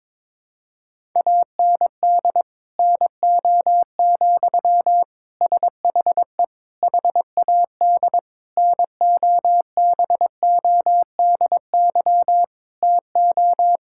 Morse code, heard as a keyed tone, about 22 wpm, 700 Hz. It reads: AND NO, SHE HAD NOBODY TO
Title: Sentences from Top 1600 Words 22wpm